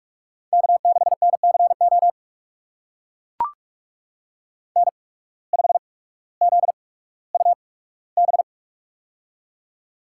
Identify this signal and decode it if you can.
Morse code, heard as a keyed tone, about 44 wpm, spaced as for 19 wpm, 700 Hz. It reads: K6NCO N5ZUB
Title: USA Call Signs - 5 Characters - 8x Character Spacing - 45wpm